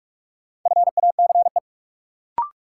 Morse code, heard as a keyed tone, about 41 wpm, 700 Hz. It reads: FACE
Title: Sentences from Top 300 Words - New Word Drill - 45wpm